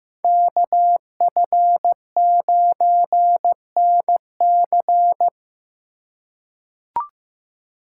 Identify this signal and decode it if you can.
Morse code, heard as a keyed tone, about 15 wpm, 700 Hz. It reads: KF9NC
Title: USA Call Signs - 5 Characters - 2x Character Spacing - 15wpm